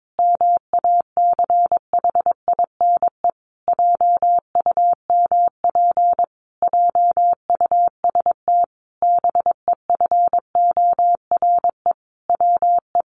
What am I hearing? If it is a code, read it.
Morse code, heard as a keyed tone, about 22 wpm, 700 Hz. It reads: MACHINE JUMP JUST BEFORE WE
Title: A Sound Of Thunder 22wpm